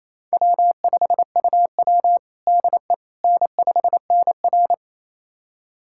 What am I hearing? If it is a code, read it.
Morse code, heard as a keyed tone, about 28 wpm, 700 Hz. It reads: W5UW DE N5NR
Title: USA Callsign DE Callsign - 4 Characters - 28wpm